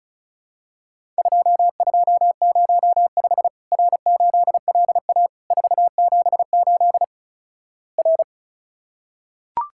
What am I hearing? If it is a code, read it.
Morse code, heard as a keyed tone, about 36 wpm, 700 Hz. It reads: 2205 R8LA 478 R
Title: Worked All Europe DX Contest - QTC Exchange - 35wpm